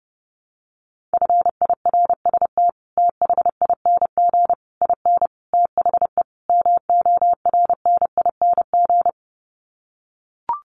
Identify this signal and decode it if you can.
Morse code, heard as a keyed tone, about 30 wpm, 700 Hz. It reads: FIRST THING IN THE MORNING
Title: Sets of 5 Words Encore2 30wpm